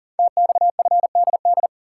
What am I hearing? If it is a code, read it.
Morse code, heard as a keyed tone, about 39 wpm, 700 Hz. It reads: TXFDD
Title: Code Groups - 5 Letters - 40wpm